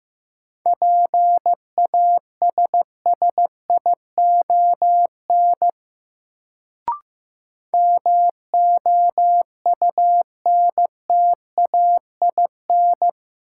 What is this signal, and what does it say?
Morse code, heard as a keyed tone, about 16 wpm, 700 Hz. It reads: PASSION MOUNTAIN
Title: Sentences from Top 1300 Words - New Word Drill Alt - 15wpm